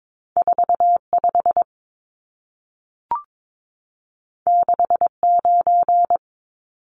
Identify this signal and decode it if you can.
Morse code, heard as a keyed tone, about 22 wpm, 700 Hz. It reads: 45 69
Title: Numbers - 2-Digit 22wpm